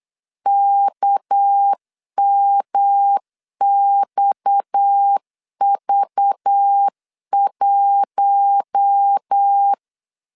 Morse code, heard as a keyed tone, about 8 wpm, 800 Hz. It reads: KMXV1
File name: LEZIONI / 6 PKX0 / 6 PKX 8wpm 3041